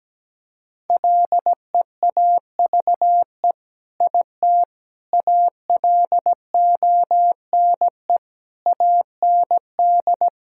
Morse code, heard as a keyed tone, about 17 wpm, 700 Hz. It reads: LEAVE IT ALONE AND
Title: Sentences from Top 1000 Words - Review Part 5 - 17wpm